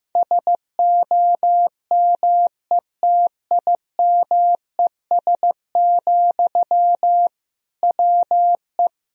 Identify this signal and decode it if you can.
Morse code, heard as a keyed tone, about 15 wpm, 700 Hz. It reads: SOMETIMES, WE
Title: Sentences from Top 1000 Words - Review Part 3 - 15wpm